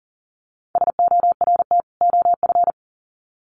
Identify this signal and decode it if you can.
Morse code, heard as a keyed tone, about 40 wpm, 700 Hz. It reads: SORT OF
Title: Sets of 2 Words 40wpm